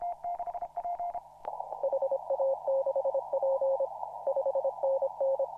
Morse code, heard as a keyed tone, about 26 wpm, 540 Hz. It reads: HA6P 5NN
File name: CW / RHOB CW 2017 HA6P 0001